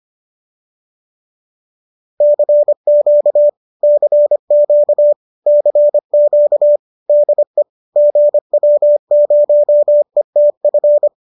Morse code, heard as a keyed tone, about 25 wpm, 590 Hz. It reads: CQ CQ CQ DE GW0ETF
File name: qso218 25